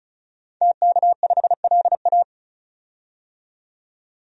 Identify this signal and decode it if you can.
Morse code, heard as a keyed tone, about 34 wpm, 700 Hz. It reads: TK5LA
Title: DXCC-100 Call Signs - 5 Characters - 35wpm